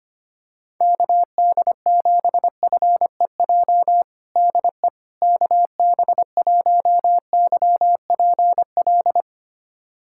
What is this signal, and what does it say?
Morse code, heard as a keyed tone, about 25 wpm, 700 Hz. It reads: KD7FEJ DE KB1YPL
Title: USA Callsign DE Callsign - 6 Characters - 25wpm